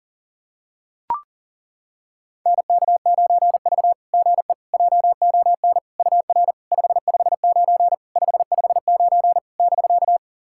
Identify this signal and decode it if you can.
Morse code, heard as a keyed tone, about 40 wpm, 690 Hz. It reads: NK9V GE JON UR 559 559 <BK>
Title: SOTA QSO Practice 40wpm